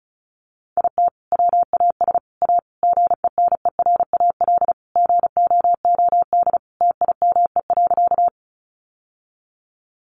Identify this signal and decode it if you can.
Morse code, heard as a keyed tone, about 34 wpm, 700 Hz. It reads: IT WAS A GENERAL GOOD TIME.
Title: Sentences from Top 600 Words 35wpm